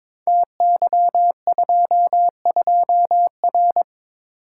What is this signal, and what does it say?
Morse code, heard as a keyed tone, about 22 wpm, 700 Hz. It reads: TY22R
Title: Code Groups - 5 Letter:Numbers - 22wpm